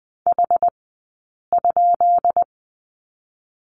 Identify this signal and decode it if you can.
Morse code, heard as a keyed tone, about 20 wpm, 700 Hz. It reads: H ?
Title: 132 Lesson 22d - Question Mark - Two Characters 20wpm